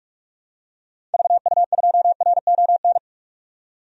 Morse code, heard as a keyed tone, about 46 wpm, 700 Hz. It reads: VU2RON
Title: DXCC-100 Call Signs - 6 Characters - 45wpm